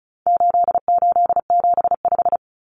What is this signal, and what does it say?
Morse code, heard as a keyed tone, about 34 wpm, 700 Hz. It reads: 8875
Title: Common Cut Numbers - 4-Digit 35wpm